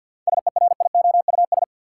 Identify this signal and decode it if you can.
Morse code, heard as a keyed tone, about 46 wpm, 700 Hz. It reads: SERIOUS
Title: Sentences from Top 1100 Words - New Word Drill Alt - 50wpm